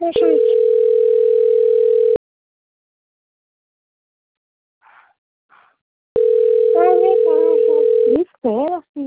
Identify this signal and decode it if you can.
Ringback tone (North America), 440 + 480 Hz, with speech.
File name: rg-1600-unknown- - - .89292